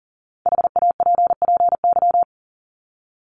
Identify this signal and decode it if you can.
Morse code, heard as a keyed tone, about 40 wpm, 700 Hz. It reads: HAPPY
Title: Top 751-1000 Words 40wpm